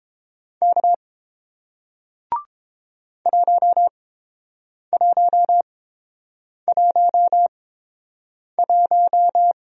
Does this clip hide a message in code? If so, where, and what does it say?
Yes: Morse code from 0.6 to 9.5 s, heard as a keyed tone, about 30 wpm, spaced as for 7 wpm, 700 Hz. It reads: K 1111